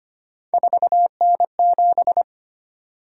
Morse code, heard as a keyed tone, about 25 wpm, 700 Hz. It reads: 4N7
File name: Three Letter-Number Combinations - 25wpm